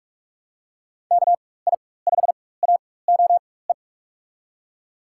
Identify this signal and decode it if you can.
Morse code, heard as a keyed tone, about 47 wpm, spaced as for 21 wpm, 700 Hz. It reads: KI5AOE
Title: USA Call Signs - 6 Characters - 4x Character Spacing - 45wpm